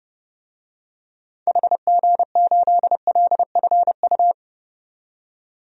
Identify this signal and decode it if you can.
Morse code, heard as a keyed tone, about 30 wpm, 700 Hz. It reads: HG8LFU
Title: DXCC-100 Call Signs - 6 Characters - 30wpm